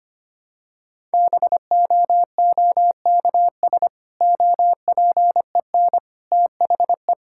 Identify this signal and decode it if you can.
Morse code, heard as a keyed tone, about 25 wpm, 700 Hz. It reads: BOOKS OPEN THE
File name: practice / 25wpm / Difficult Sentences with Hints - 25wpm